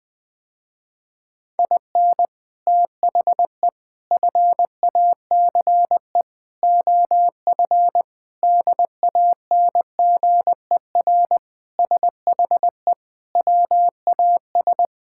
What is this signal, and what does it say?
Morse code, heard as a keyed tone, about 20 wpm, 700 Hz. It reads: IN THE FACE OF DANGER SHE WAS
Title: Sentences with 4-Word Phrases - Spoken Beginning - 20wpm